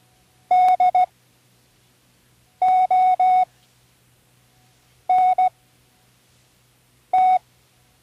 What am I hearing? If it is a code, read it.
Morse code, heard as a keyed tone, about 16 wpm, spaced as for 5 wpm, 740 Hz. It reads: DONT
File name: Learning Morse Code (from Tune in the World)